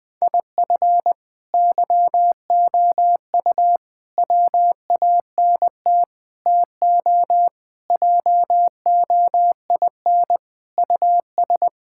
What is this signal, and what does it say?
Morse code, heard as a keyed tone, about 20 wpm, 700 Hz. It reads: IF YOU WANT TO JOIN US
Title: Sentences with 4-Word Phrases - Spoken Beginning - 20wpm